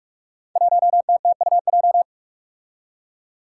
Morse code, heard as a keyed tone, about 46 wpm, 700 Hz. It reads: 1TTUJ
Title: Five Challenging Letter-Number Combinations 45wpm